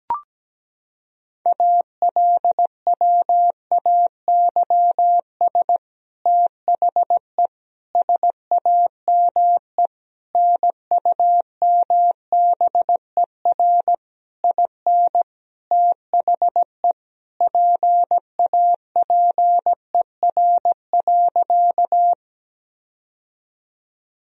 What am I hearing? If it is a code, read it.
Morse code, heard as a keyed tone, about 17 wpm, 700 Hz. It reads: ALWAYS THE SAME NUMBER IN THE PAPER.
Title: Sentences from Top 1000 Words - Review Part 1 - 17wpm